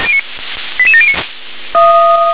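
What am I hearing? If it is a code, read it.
Five-tone selective call (ZVEI), heard as quick runs of five tones, tones of about 70 ms. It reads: 88709
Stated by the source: ZVEI five-tone selective call received on SDR.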